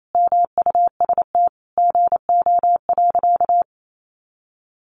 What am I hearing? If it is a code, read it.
Morse code, heard as a keyed tone, about 28 wpm, 700 Hz. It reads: MUST GO.
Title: Sentences from Top 1200 Words 28wpm